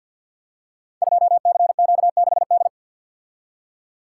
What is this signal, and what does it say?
Morse code, heard as a keyed tone, about 50 wpm, 700 Hz. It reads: 2CQ6D
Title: Code Groups - 5 Letter:Numbers - 50wpm